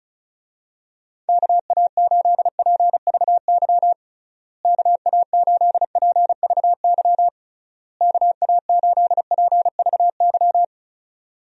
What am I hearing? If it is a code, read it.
Morse code, heard as a keyed tone, about 34 wpm, 700 Hz. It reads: KA8PVY KA8PVY KA8PVY
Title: USA Call Signs - 6 Characters - Repeated 3x - 35wpm